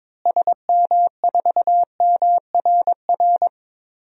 Morse code, heard as a keyed tone, about 22 wpm, 700 Hz. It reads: SM4MRR
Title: DXCC-100 Call Signs - 6 Characters - Repeated 2x - 22wpm